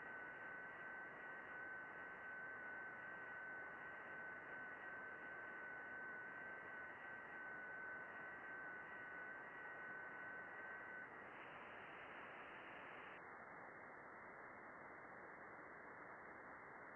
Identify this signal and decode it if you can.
FT8.
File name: contests / june-vhf-k5tr- -0200